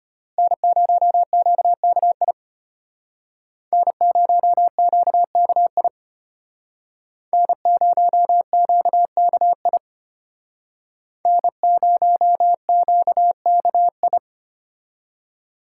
Morse code, heard as a keyed tone, about 32 wpm, 700 Hz. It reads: N0QKI N0QKI N0QKI N0QKI